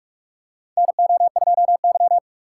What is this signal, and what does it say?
Morse code, heard as a keyed tone, about 47 wpm, 700 Hz. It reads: NO2Y
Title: USA Call Signs - 4 Characters - 2x Character Spacing - 45wpm